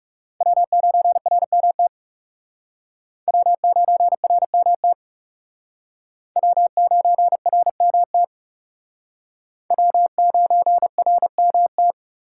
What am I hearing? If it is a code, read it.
Morse code, heard as a keyed tone, about 40 wpm, 700 Hz. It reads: W9RMT W9RMT W9RMT W9RMT